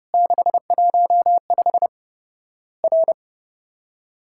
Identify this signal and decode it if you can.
Morse code, heard as a keyed tone, about 30 wpm, 700 Hz. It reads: 615 R
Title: Worked All Europe DX Contest - QTC Exchange - 30wpm